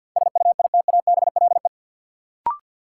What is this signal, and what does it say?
Morse code, heard as a keyed tone, about 51 wpm, 700 Hz. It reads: SUITABLE
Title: Sentences from Top 1400 Words - New Word Drill - 50wpm